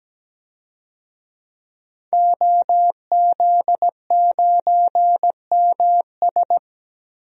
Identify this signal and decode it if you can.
Morse code, heard as a keyed tone, about 17 wpm, 700 Hz. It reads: OZ9MS
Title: DXCC-100 Call Signs - 5 Characters - 17wpm